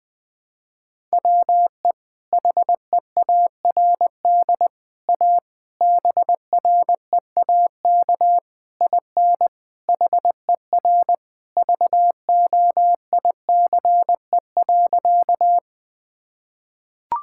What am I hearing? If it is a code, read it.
Morse code, heard as a keyed tone, about 21 wpm, 700 Hz. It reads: WE HEARD A BREAK IN HER VOICE.
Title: Sentences from Top 1000 Words - Review Part 6 - 20wpm